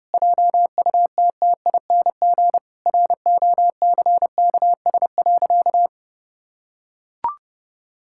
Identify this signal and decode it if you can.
Morse code, heard as a keyed tone, about 30 wpm, 700 Hz. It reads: JUTTING ROCKS.
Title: Pangram Challenge v2 - 30wpm